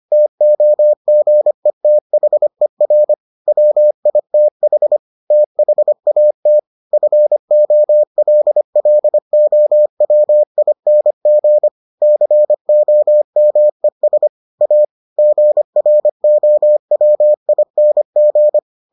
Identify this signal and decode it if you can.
Morse code, heard as a keyed tone, about 25 wpm, 600 Hz. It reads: TOGETHER WITH THAT FOLLOWING COMES A GROWING